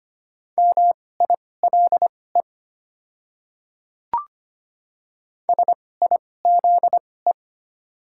Morse code, heard as a keyed tone, about 25 wpm, spaced as for 18 wpm, 700 Hz. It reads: MILE SIZE
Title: 2x Character Spacing - Top 500 Words - 25wpm